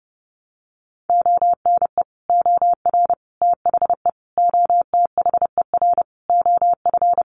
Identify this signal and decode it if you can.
Morse code, heard as a keyed tone, about 30 wpm, 700 Hz. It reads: ONE OR THE OTHER OF
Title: Sentences from Top 400 Words 30wpm